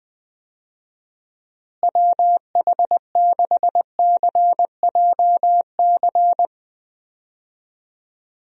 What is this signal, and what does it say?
Morse code, heard as a keyed tone, about 20 wpm, 700 Hz. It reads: WH6CJC
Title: USA Call Signs - 6 Characters - Repeated 3x - 20wpm